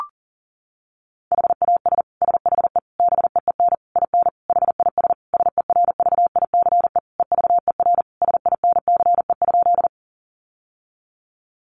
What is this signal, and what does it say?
Morse code, heard as a keyed tone, about 40 wpm, 700 Hz. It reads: HAS SHE BEEN IN HIS SERVICE EVER SINCE?
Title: Sentences from Top 700 Words 40wpm